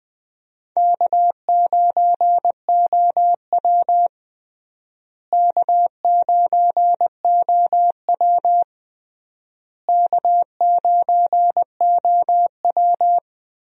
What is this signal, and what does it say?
Morse code, heard as a keyed tone, about 20 wpm, 700 Hz. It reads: K9OW K9OW K9OW
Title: USA Call Signs - 4 Characters - Repeated 3x - 20wpm